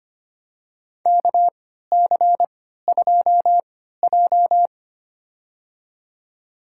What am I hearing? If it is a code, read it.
Morse code, heard as a keyed tone, about 25 wpm, spaced as for 14 wpm, 700 Hz. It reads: KC2J
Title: USA Call Signs - 4 Characters - 3x Character Spacing - 25wpm